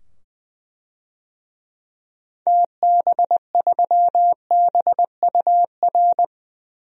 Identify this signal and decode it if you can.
Morse code, heard as a keyed tone, about 20 wpm, 700 Hz. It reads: TB3BUR
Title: DXCC-100 Call Signs - 6 Characters - 20wpm